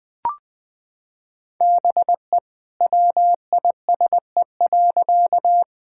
Morse code, heard as a keyed tone, about 20 wpm, 700 Hz. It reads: BE WISE.
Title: Sentences from Top 1600 Words 20wpm